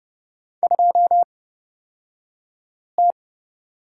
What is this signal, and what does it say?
Morse code, heard as a keyed tone, about 29 wpm, spaced as for 5 wpm, 700 Hz. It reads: 2T
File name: Lesson 36d - BT - Two Characters 30wpm